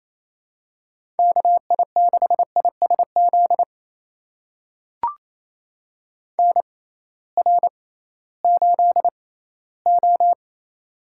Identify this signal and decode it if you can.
Morse code, heard as a keyed tone, about 28 wpm, spaced as for 16 wpm, 700 Hz. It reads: KI6ISZ NR8O
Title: USA Call Signs - 6x Character Spacing - 28wpm